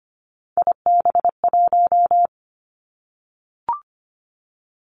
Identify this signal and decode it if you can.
Morse code, heard as a keyed tone, about 25 wpm, 700 Hz. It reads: IB1
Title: Three Challenging Letter-Number Combinations 25wpm